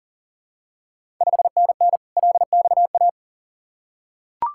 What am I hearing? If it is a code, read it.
Morse code, heard as a keyed tone, about 40 wpm, 700 Hz. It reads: 5NN LXA
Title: Ontario QSO Party Exchange 40wpm